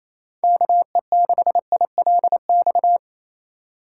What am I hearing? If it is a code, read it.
Morse code, heard as a keyed tone, about 28 wpm, 700 Hz. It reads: KE6ILX
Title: USA Call Signs - 6 Characters - 4x Character Spacing - 28wpm